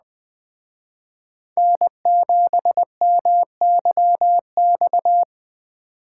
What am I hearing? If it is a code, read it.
Morse code, heard as a keyed tone, about 20 wpm, 700 Hz. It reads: N7MYX